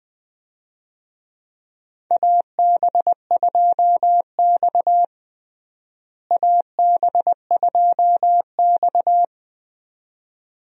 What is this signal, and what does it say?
Morse code, heard as a keyed tone, about 20 wpm, 700 Hz. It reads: AB2X AB2X
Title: USA Call Signs - 4 Characters - Repeated 2x - 20wpm